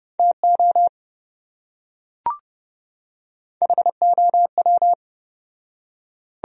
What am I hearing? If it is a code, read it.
Morse code, heard as a keyed tone, about 28 wpm, 700 Hz. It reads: TO HOW